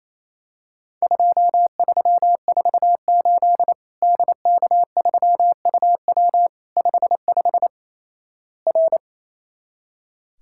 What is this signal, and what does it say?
Morse code, heard as a keyed tone, about 28 wpm, 700 Hz. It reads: 2348 DK3UW 55 R
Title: Worked All Europe DX Contest - QTC Exchange - 28wpm